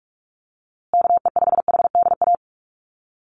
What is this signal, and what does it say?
Morse code, heard as a keyed tone, about 44 wpm, 700 Hz. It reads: KE5HDA
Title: USA Call Signs Repeated 2x 45wpm